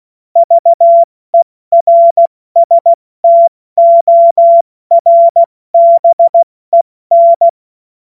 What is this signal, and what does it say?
Morse code, heard as a keyed tone, about 16 wpm, 680 Hz. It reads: VERSTORBEN
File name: Morsekurs / Witze / Witz 09. Dezember 2025 70 F80